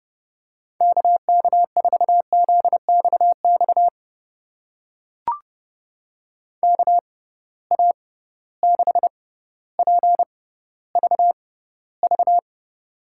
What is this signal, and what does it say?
Morse code, heard as a keyed tone, about 31 wpm, spaced as for 15 wpm, 700 Hz. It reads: KK4ZXX KA6PVV
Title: USA Call Signs - 6 Characters - 6x Character Spacing - 30wpm